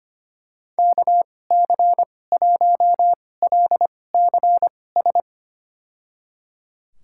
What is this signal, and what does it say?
Morse code, heard as a keyed tone, about 25 wpm, spaced as for 18 wpm, 700 Hz. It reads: KC1LCS
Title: USA Call Signs - 6 Characters - 2x Character Spacing - 25wpm